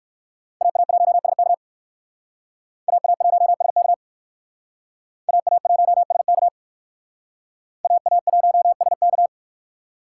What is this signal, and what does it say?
Morse code, heard as a keyed tone, about 56 wpm, 700 Hz. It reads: AA1SK AA1SK AA1SK AA1SK